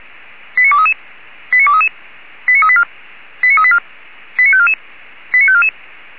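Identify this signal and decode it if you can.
Five-tone selective call (ZVEI), heard as quick runs of five tones, tones of about 70 ms. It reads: 89233 89233 89384 89384 89544 89544
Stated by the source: ZVEI five-tone selective call received by SDR, from audio.f85360.feuerwehr-orlamuende.org (Feuerwehr Orlamünde fire brigade).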